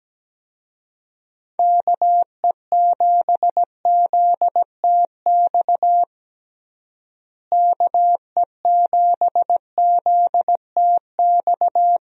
Morse code, heard as a keyed tone, about 17 wpm, 700 Hz. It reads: KE7ZTX KE7ZTX
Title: USA Call Signs - 6 Characters - Repeated 3x - 17wpm